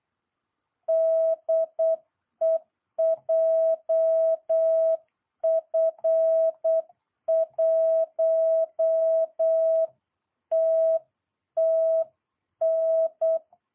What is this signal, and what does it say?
Morse code, heard as a keyed tone, about 8 wpm, 660 Hz. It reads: DE JF1TTN